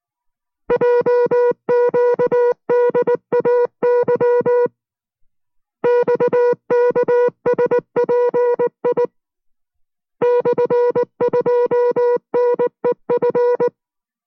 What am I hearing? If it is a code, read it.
Morse code, heard as a keyed tone, about 19 wpm, 490 Hz. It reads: JQDAY XKSPI /2NEF